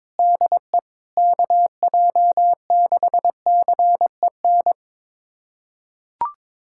Morse code, heard as a keyed tone, about 22 wpm, 700 Hz. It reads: DE KJ6CEN
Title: USA Callsign DE Callsign - 6 Characters - 22wpm